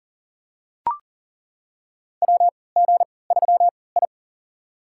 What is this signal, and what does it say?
Morse code, heard as a keyed tone, about 40 wpm, spaced as for 23 wpm, 700 Hz. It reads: WG3I
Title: USA Call Signs - 4 Characters - 3x Character Spacing - 40wpm